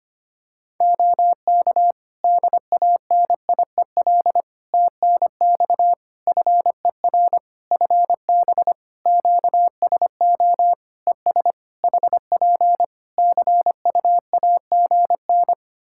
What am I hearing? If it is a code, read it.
Morse code, heard as a keyed tone, about 25 wpm, 700 Hz. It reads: OK DANIEL TNX FER FB QSO ES HP CUAGN
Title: Practice QSO - Exchange 4 - Ending - 25wpm